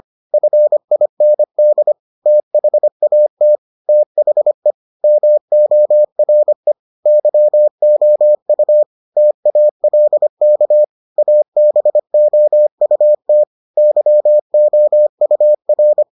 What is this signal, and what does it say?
Morse code, heard as a keyed tone, about 25 wpm, 600 Hz. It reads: FIND THAT THE MORE YOU TALK ABOUT YOUR